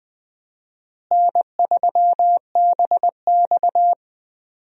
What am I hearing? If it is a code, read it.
Morse code, heard as a keyed tone, about 20 wpm, 700 Hz. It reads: N3BX
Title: USA Call Signs - 4 Characters - Repeated 2x - 20wpm